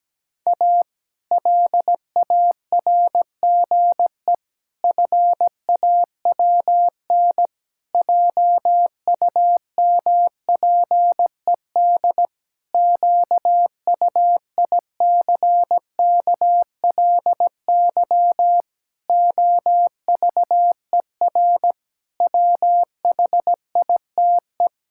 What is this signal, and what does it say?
Morse code, heard as a keyed tone, about 17 wpm, 700 Hz. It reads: A LARGE FAWN JUMPED QUICKLY OVER WHITE
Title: Pangram Challenge v2 - 17wpm